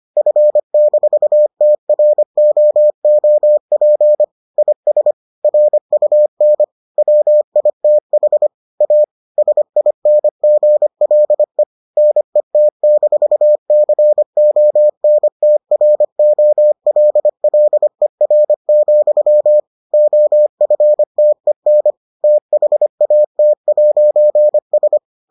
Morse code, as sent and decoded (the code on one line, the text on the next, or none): ..-. -....- - .-. --- --- .--. / .. ... / .-. ..- -. / .-- .. - .... / .- / ... .. -. --. .-.. / -. - -....- -.-. --- -. - .-. --- .-.. .-.. .-. --..-- / --- ..-. - . -. / - .... .- - .----. ...
F-TROOP IS RUN WITH A SINGLE NET-CONTROLLER, OFTEN THAT'S